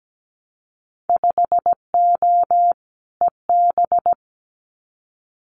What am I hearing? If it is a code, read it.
Morse code, heard as a keyed tone, about 17 wpm, 700 Hz. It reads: 5O EB
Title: Winter Field Day Exchanges Encore 17wpm